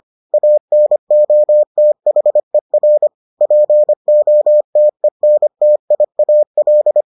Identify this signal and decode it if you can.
Morse code, heard as a keyed tone, about 25 wpm, 600 Hz. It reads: ANOTHER POTENTIAL